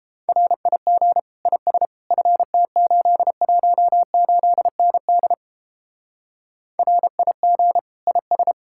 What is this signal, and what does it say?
Morse code, heard as a keyed tone, about 29 wpm, 700 Hz. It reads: RIG IS FT818ND RIG IS